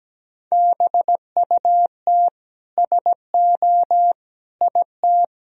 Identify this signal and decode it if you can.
Morse code, heard as a keyed tone, about 17 wpm, 700 Hz. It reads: BUT SO IT
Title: Sentences from Top 1000 Words - Review Part 4 - 17wpm